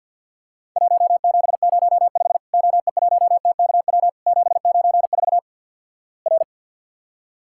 Morse code, heard as a keyed tone, about 52 wpm, 700 Hz. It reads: 1705 OE1TKW 794 R
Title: Worked All Europe DX Contest - QTC Exchange - 50wpm